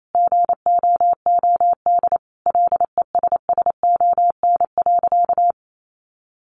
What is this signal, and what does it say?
Morse code, heard as a keyed tone, about 28 wpm, 700 Hz. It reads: GOOD LESSON.
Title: Sentences from Top 1200 Words 28wpm